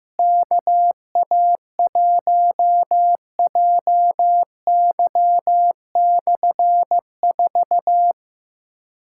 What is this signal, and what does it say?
Morse code, heard as a keyed tone, about 15 wpm, 700 Hz. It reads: KA1JY/4
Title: K1USN SST QSO Practice - 15wpm